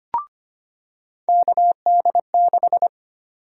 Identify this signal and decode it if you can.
Morse code, heard as a keyed tone, about 25 wpm, 700 Hz. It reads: KD6
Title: USA Call Sign Prefixes - 25wpm